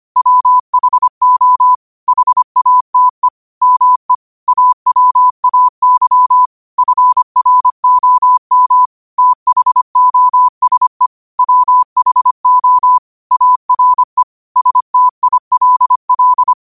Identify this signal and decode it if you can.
Morse code, heard as a keyed tone, about 25 wpm, 1000 Hz. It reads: WHO HATE ME AWAY FROM THOSE WHO ARE STILL
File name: qotdfiles / 25wpm